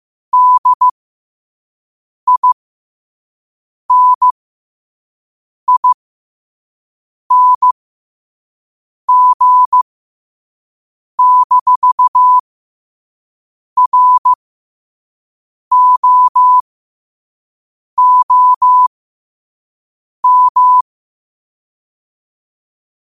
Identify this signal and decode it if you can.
Morse code, heard as a keyed tone, about 15 wpm, spaced as for 5 wpm, 1000 Hz. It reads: DINING-ROOM
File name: qotdfiles / 5wpm